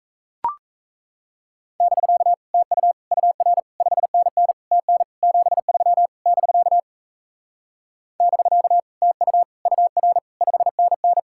Morse code, heard as a keyed tone, about 38 wpm, 700 Hz. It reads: <BK> TU UR 5NN TN 73 <BK> <BK> TU UR 5NN